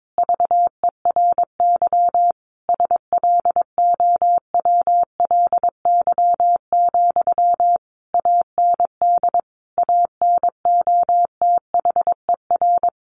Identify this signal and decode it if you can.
Morse code, heard as a keyed tone, about 22 wpm, 700 Hz. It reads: VERY SLOWLY, AND ANOTHER
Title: Sentences from Top 1100 Words 22wpm